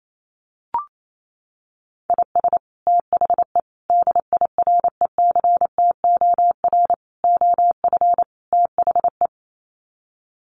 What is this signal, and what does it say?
Morse code, heard as a keyed tone, about 28 wpm, 700 Hz. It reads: IS THE DIRECTOR OF THE
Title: Sets of 5 Words Encore2 28wpm